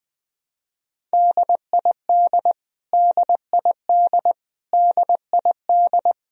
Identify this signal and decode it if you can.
Morse code, heard as a keyed tone, about 20 wpm, 700 Hz. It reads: DID DID DID
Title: Top 100 Words Repeated 20wpm